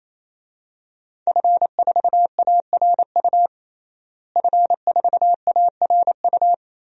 Morse code, heard as a keyed tone, about 27 wpm, 700 Hz. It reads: F4ARU F4ARU
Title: DXCC-100 Call Signs - 5 Characters - Repeated 3x - 28wpm